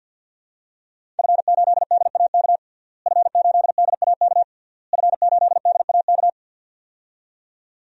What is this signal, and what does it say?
Morse code, heard as a keyed tone, about 50 wpm, 700 Hz. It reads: F8DAK F8DAK F8DAK
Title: DXCC-100 Call Signs - 5 Characters - Repeated 3x - 50wpm